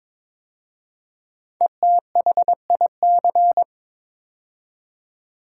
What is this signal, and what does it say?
Morse code, heard as a keyed tone, about 22 wpm, 700 Hz. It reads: ETHIC
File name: 5 Letter Words - First 3 Letters Given - 22wpm